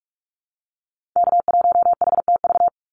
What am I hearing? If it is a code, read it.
Morse code, heard as a keyed tone, about 44 wpm, 700 Hz. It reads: K1HTV
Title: Call Signs - 45wpm